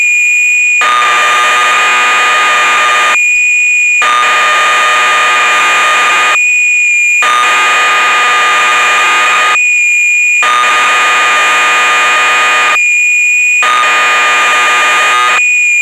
Cassette data (Kansas City type).
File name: magazines / Acorn User / 1986-04 Electron Programs